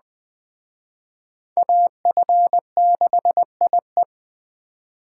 Morse code, heard as a keyed tone, about 20 wpm, 700 Hz. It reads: AF6IE
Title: Speed Racing - USA 2x2 Call Signs 17wpm